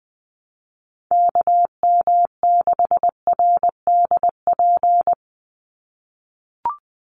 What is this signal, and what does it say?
Morse code, heard as a keyed tone, about 20 wpm, 700 Hz. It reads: KM6RDP
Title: USA Call Signs 20wpm